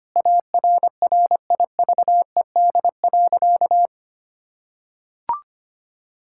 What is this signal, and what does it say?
Morse code, heard as a keyed tone, about 25 wpm, 700 Hz. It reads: ARRIVED.
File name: practice / 25wpm / Sentences with 3-Word Phrases - 25wpm